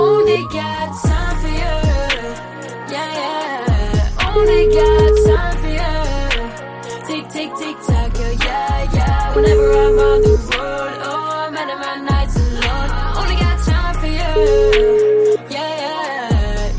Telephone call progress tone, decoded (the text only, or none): ringback tone